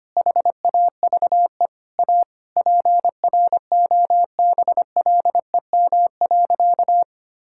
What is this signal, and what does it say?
Morse code, heard as a keyed tone, about 25 wpm, 700 Hz. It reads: HAVE A PROBLEM.
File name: practice / 25wpm / Sentences with 4-Word Phrases - 25wpm